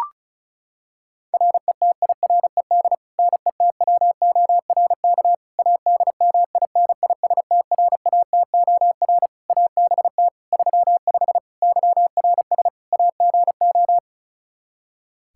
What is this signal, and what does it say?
Morse code, heard as a keyed tone, about 36 wpm, 700 Hz. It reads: RETIRED NETWORK ADMINISTRATOR ABT 35 YRS AGO
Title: Practice QSO - Exchange 3 - More Info - 35wpm